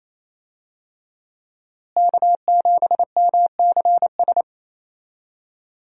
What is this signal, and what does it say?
Morse code, heard as a keyed tone, about 27 wpm, 700 Hz. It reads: K7MCS
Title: USA Call Signs - 5 Characters - Repeated 3x - 28wpm